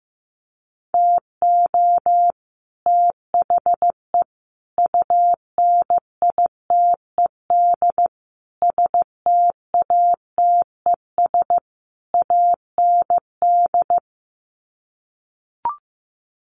Morse code, heard as a keyed tone, about 14 wpm, 700 Hz. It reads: TO THE UNITED STATES AND
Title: Sets of 5 Words Encore 15wpm